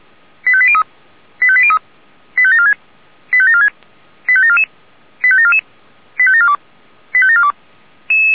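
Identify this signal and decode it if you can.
Five-tone selective call (ZVEI), heard as quick runs of five tones, tones of about 70 ms. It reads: 85803 85803 85746 85746 85744 85744 85742 85742
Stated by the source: ZVEI five-tone selective call received by SDR.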